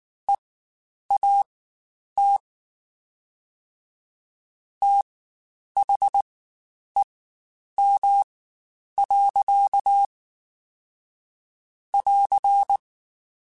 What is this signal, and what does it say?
Morse code, heard as a keyed tone, about 19 wpm, spaced as for 9 wpm, 800 Hz. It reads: EAT THEM. <AR>
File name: Spaced / Chuckles Spaced-mp3s / SoftWords sp @18